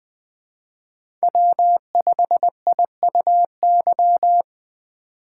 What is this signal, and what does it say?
Morse code, heard as a keyed tone, about 24 wpm, 700 Hz. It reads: W5IUY
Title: Speed Racing - USA 1x3 Call Signs 20wpm